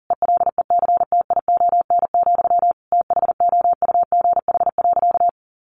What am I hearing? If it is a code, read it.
Morse code, heard as a keyed tone, about 40 wpm, 700 Hz. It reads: ELECTION, THOUGH.
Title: Sentences from Top 1500 Words 40wpm